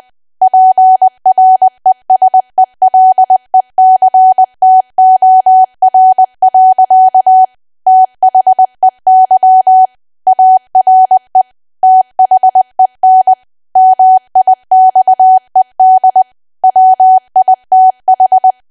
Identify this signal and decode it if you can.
Morse code, heard as a keyed tone, about 20 wpm, 750 Hz. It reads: PRESELECTOR. THEY ARE THEN MIXED WITH
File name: cwa / W1AW-Level3 / 20WPM